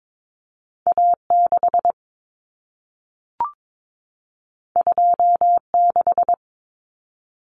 Morse code, heard as a keyed tone, about 22 wpm, 700 Hz. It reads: A6 26
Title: Common Cut Numbers - 2-Digit 22wpm